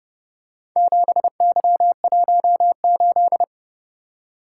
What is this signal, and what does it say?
Morse code, heard as a keyed tone, about 30 wpm, 700 Hz. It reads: 7Y18